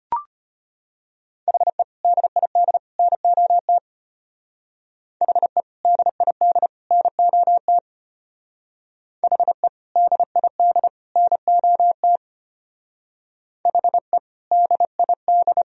Morse code, heard as a keyed tone, about 34 wpm, 700 Hz. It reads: HE DID NOT HE DID NOT HE DID NOT HE DID